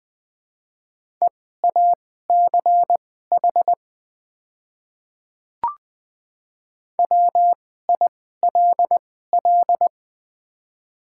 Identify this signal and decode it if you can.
Morse code, heard as a keyed tone, about 21 wpm, spaced as for 15 wpm, 700 Hz. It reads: EACH WILL
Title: 2x Character Spacing - Top 300 Words - 20wpm